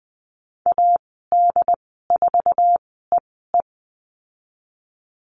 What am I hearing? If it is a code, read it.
Morse code, heard as a keyed tone, about 20 wpm, spaced as for 14 wpm, 700 Hz. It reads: AD4EE
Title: 082 Lesson 14g - Number 2 - Call Signs 20wpm